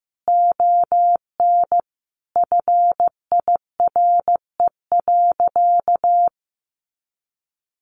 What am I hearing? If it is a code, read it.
Morse code, heard as a keyed tone, about 15 wpm, 700 Hz. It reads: ON FIRE.
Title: Sentences from Top 1400 Words 15wpm